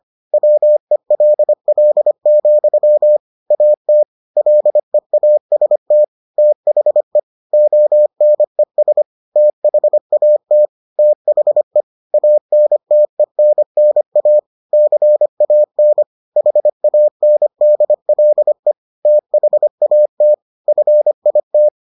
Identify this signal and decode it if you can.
Morse code, heard as a keyed tone, about 25 wpm, 600 Hz. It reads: WELL, AT LEAST THE ONES THAT THE ANTENNA CAN HANDLE THAT FIT